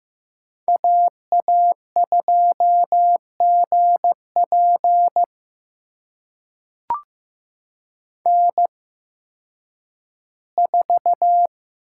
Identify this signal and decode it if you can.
Morse code, heard as a keyed tone, about 15 wpm, spaced as for 10 wpm, 700 Hz. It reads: AA2GP N4
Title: USA Call Signs - 5 Characters - 8x Character Spacing - 15wpm